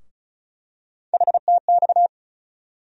Morse code, heard as a keyed tone, about 36 wpm, 700 Hz. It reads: HTX
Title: USA Call Sign Postfixes - 3 Letters 35wpm